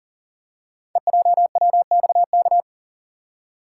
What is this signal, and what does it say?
Morse code, heard as a keyed tone, about 38 wpm, 700 Hz. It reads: EJWXK
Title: Code Groups - 5 Letters - 40wpm